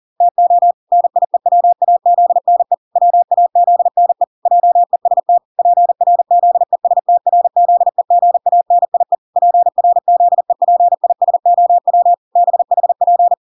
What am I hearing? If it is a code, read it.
Morse code, heard as a keyed tone, about 40 wpm, 700 Hz. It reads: TO NIEWAZNE WAZNE JEST PRZESTRZEGANIE PRZEPISOW BHP